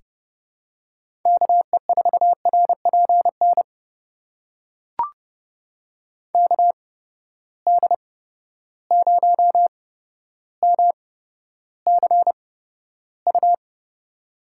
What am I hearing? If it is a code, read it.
Morse code, heard as a keyed tone, about 30 wpm, spaced as for 13 wpm, 700 Hz. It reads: KE4RPN KD0MCU
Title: USA Call Signs - 6 Characters - 8x Character Spacing - 30wpm